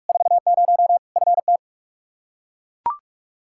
Morse code, heard as a keyed tone, about 41 wpm, 700 Hz. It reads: 40 FT
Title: Practice QSO - Exchange 2 - Info - 45wpm